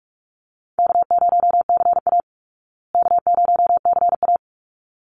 Morse code, heard as a keyed tone, about 44 wpm, 700 Hz. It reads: K0CA K0CA
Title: Easy Call Signs Repeated 2x - 45wpm